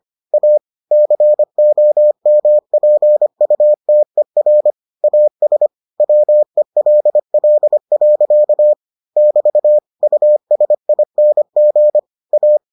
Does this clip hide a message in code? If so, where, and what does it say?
Yes: Morse code from 0.3 to 12.6 s, heard as a keyed tone, about 25 wpm, 600 Hz. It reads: A COMPUTER AS WELL. <BT> USING A